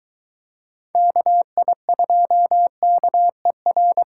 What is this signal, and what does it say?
Morse code, heard as a keyed tone, about 23 wpm, 700 Hz. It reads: KI2KER